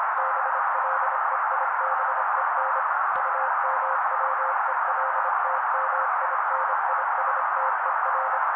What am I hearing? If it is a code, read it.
Morse code, heard as a keyed tone, about 26 wpm, 530 Hz. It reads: BLEIBEN UMWELTMINISTER
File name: daily / 25wpm noise